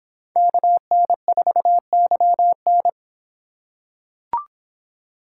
Morse code, heard as a keyed tone, about 25 wpm, 700 Hz. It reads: KN4YN